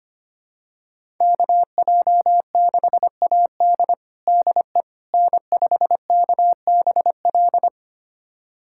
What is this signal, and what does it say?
Morse code, heard as a keyed tone, about 25 wpm, 700 Hz. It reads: KJ6AD DE N5KBL
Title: USA Callsign DE Callsign - 5 Characters - 25wpm